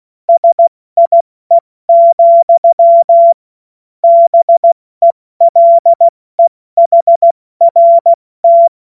Morse code, heard as a keyed tone, about 16 wpm, 680 Hz. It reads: SIE, BELEHRT
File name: Morsekurs / Witze / Witz 21. Februar 2023 70 F80